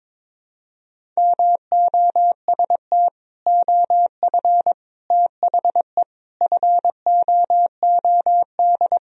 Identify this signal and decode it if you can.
Morse code, heard as a keyed tone, about 22 wpm, 700 Hz. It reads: MOST OF THE FOOD
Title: Sentences from Top 1000 Words - Review Part 1 - 22wpm